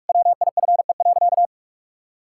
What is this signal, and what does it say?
Morse code, heard as a keyed tone, about 43 wpm, 700 Hz. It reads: WIFE.
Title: Pangram Challenge v2 - 45wpm